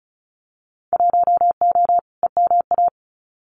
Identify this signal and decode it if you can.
Morse code, heard as a keyed tone, about 36 wpm, 700 Hz. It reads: 1O EMA
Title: Winter Field Day Exchanges 35wpm